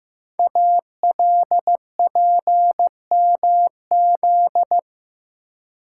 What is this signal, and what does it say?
Morse code, heard as a keyed tone, about 15 wpm, 700 Hz. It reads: ALPMZ
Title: Code Groups - 5 Letter:Numbers - 15wpm